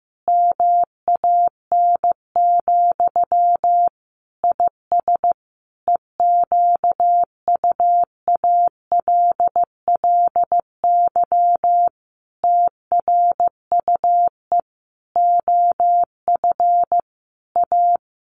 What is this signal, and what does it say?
Morse code, heard as a keyed tone, about 15 wpm, 700 Hz. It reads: MAN, IS EQUALLY TRUE OF A
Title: Sentences from Top 1400 Words 15wpm